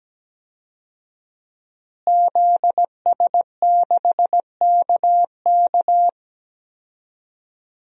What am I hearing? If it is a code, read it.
Morse code, heard as a keyed tone, about 17 wpm, 700 Hz. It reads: ZS6KK
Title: DXCC-100 Call Signs - 5 Characters - Repeated 3x - 17wpm